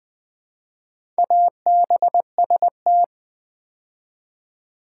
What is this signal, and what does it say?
Morse code, heard as a keyed tone, about 20 wpm, 700 Hz. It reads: ABST
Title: Build A Word - 2 Letters At A Time - 8-Letter Words - 20wpm